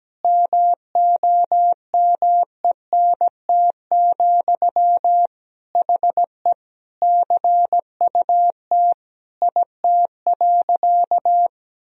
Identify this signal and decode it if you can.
Morse code, heard as a keyed tone, about 17 wpm, 700 Hz. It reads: MOMENT, HE CUT IT.
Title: Sentences from Top 1000 Words - Review Part 1 - 17wpm